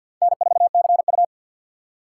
Morse code, heard as a keyed tone, about 51 wpm, 700 Hz. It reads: N4CU
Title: USA Call Signs - 4 Characters - 2x Character Spacing - 50wpm